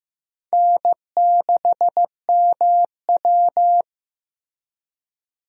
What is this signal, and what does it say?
Morse code, heard as a keyed tone, about 15 wpm, 700 Hz. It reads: N6MW
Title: USA Call Signs - 4 Characters - Repeated 3x - 15wpm